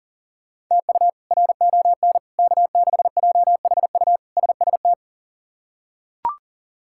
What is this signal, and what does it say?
Morse code, heard as a keyed tone, about 40 wpm, 700 Hz. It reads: TU RON K6JHU SST
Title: K1USN SST QSO Practice - 40wpm